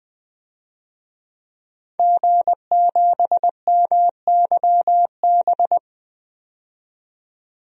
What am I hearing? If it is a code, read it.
Morse code, heard as a keyed tone, about 21 wpm, 700 Hz. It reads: G7MYB
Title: DXCC-100 Call Signs - 5 Characters - 20wpm